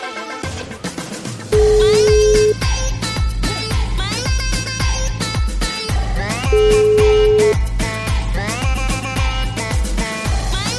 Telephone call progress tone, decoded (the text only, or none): ringback tone